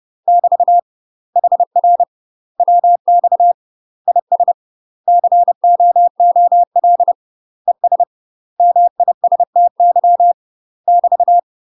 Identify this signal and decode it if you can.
Morse code, heard as a keyed tone, about 30 wpm, 700 Hz. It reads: <BT> HR WX IS COOL ES MISTY <BT>
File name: UK MASTER QSOs TRAINING / 30WPM / UK MASTER QSOs 0004